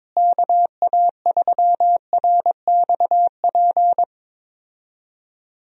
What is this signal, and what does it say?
Morse code, heard as a keyed tone, about 22 wpm, 700 Hz. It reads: KA3RXP
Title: USA Call Signs - 6 Characters - Repeated 3x - 22wpm